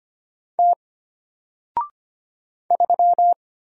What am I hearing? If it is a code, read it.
Morse code, heard as a keyed tone, about 25 wpm, 700 Hz. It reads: T 3
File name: Single-Letter-Number-25wpm